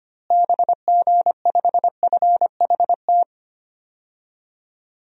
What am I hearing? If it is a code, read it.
Morse code, heard as a keyed tone, about 25 wpm, 700 Hz. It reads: BG5FHT
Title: DXCC-100 Call Signs - 6 Characters - Repeated 3x - 25wpm